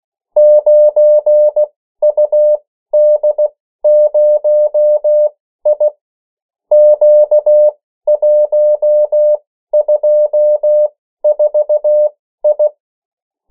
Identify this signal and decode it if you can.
Morse code, heard as a keyed tone, about 16 wpm, spaced as for 13 wpm, 600 Hz. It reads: 9UD0I Q124I
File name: morse mp3 / 14 urok 13